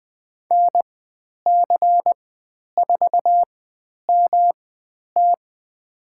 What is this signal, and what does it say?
Morse code, heard as a keyed tone, about 20 wpm, spaced as for 10 wpm, 700 Hz. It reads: NC4MT
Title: 109 Lesson 18g - Letter m - Call Signs 20wpm